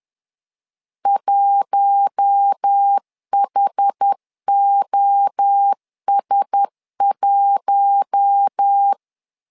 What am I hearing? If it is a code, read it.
Morse code, heard as a keyed tone, about 10 wpm, 800 Hz. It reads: 1HOS1
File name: LEZIONI / 4 Q1S 10wpm 2133